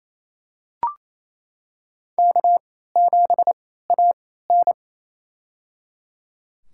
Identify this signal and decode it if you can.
Morse code, heard as a keyed tone, about 28 wpm, spaced as for 16 wpm, 700 Hz. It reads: K7AN
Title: USA Call Signs - 4 Characters - 3x Character Spacing - 28wpm